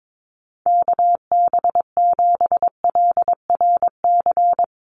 Morse code, heard as a keyed tone, about 22 wpm, 700 Hz. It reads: KB7LRC
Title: USA Call Signs 22wpm